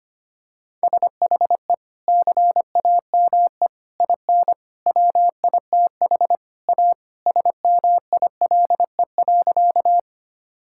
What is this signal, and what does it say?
Morse code, heard as a keyed tone, about 25 wpm, 700 Hz. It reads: SHE CAME IN WITH A SMILE.
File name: practice / 25wpm / Sentences with 2-Word Phrases - 25wpm